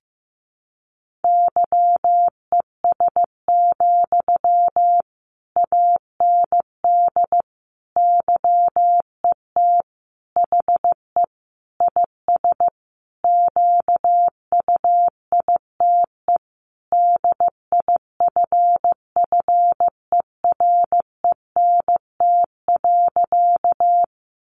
Morse code, heard as a keyed tone, about 15 wpm, 700 Hz. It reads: YES, AND YET HE IS QUITE DIFFERENT.
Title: Sentences from Top 700 Words 15wpm